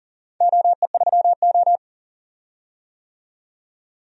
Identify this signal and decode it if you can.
Morse code, heard as a keyed tone, about 40 wpm, 700 Hz. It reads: OE3O
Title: DXCC-100 Call Signs - 4 Characters - 40wpm